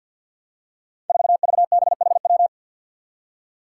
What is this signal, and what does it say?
Morse code, heard as a keyed tone, about 53 wpm, 700 Hz. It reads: 4VBHW
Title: Five Challenging Letter-Number Combinations 50wpm